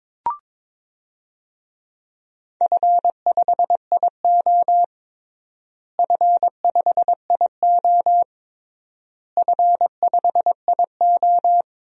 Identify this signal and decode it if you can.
Morse code, heard as a keyed tone, about 22 wpm, 700 Hz. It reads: F5IO F5IO F5IO
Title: DXCC-100 Call Signs - 4 Characters - Repeated 3x - 22wpm